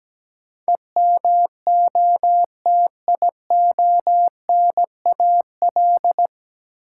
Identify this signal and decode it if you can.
Morse code, heard as a keyed tone, about 17 wpm, 700 Hz. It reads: EMOTIONAL
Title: Sentences from Top 1000 Words - Review Part 5 - New Word Drill - 17wpm